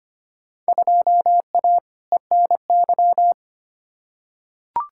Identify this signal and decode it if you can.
Morse code, heard as a keyed tone, about 25 wpm, 700 Hz. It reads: 2A ENY
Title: Field Day Exchanges - 25wpm